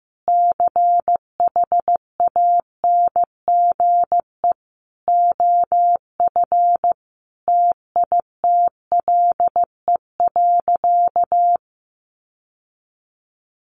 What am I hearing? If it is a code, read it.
Morse code, heard as a keyed tone, about 15 wpm, 700 Hz. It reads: CHANGE OF TITLE.
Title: Sentences from Top 1200 Words 15wpm